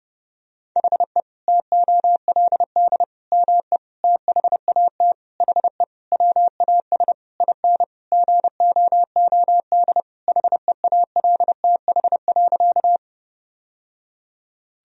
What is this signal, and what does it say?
Morse code, heard as a keyed tone, about 32 wpm, 700 Hz. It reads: HE TOLD ME THAT HE WAS IN GOOD HEALTH.
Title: Sentences from Top 1000 Words - Review Part 3 - 30wpm